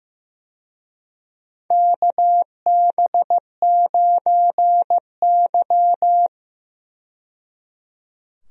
Morse code, heard as a keyed tone, about 15 wpm, 700 Hz. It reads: KB9Y
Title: USA Call Signs - 4 Characters - 15wpm